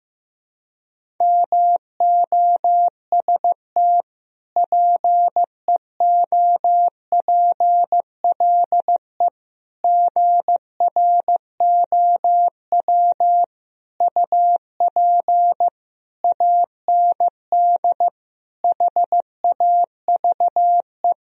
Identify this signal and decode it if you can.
Morse code, heard as a keyed tone, about 16 wpm, 700 Hz. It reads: MOST PEOPLE GROW UP AND HAVE
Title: Sentences from Top 1000 Words - Review Part 1 - 15wpm